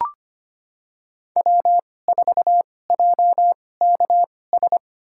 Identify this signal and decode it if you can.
Morse code, heard as a keyed tone, about 25 wpm, spaced as for 18 wpm, 700 Hz. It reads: W4JKS
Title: USA Call Signs - 5 Characters - 2x Character Spacing - 25wpm